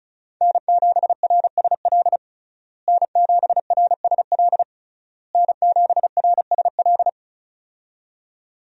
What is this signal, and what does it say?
Morse code, heard as a keyed tone, about 34 wpm, 700 Hz. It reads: N7RSL N7RSL N7RSL
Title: USA Call Signs - 5 Characters - Repeated 3x - 35wpm